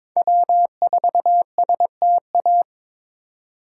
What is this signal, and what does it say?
Morse code, heard as a keyed tone, about 22 wpm, 700 Hz. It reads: W4STA
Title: USA Call Signs - 5 Characters - 2x Character Spacing - 22wpm